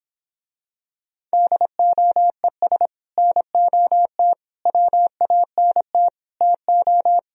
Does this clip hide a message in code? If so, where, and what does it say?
Yes: Morse code from 1.3 to 7.2 s, heard as a keyed tone, about 26 wpm, 700 Hz. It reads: DOES NOT WANT TO